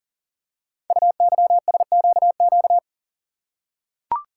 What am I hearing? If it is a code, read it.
Morse code, heard as a keyed tone, about 40 wpm, 700 Hz. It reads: UYSQQ
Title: Code Groups - 5 Letters - 40wpm